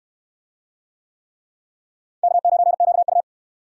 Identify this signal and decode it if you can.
Morse code, heard as a keyed tone, about 67 wpm, 700 Hz. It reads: K9QU